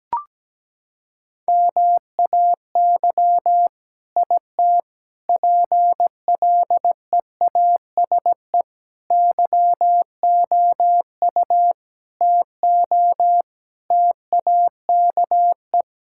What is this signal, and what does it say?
Morse code, heard as a keyed tone, about 17 wpm, 700 Hz. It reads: MAY IT PLEASE YOU TO TAKE
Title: Sentences from Top 1600 Words 17wpm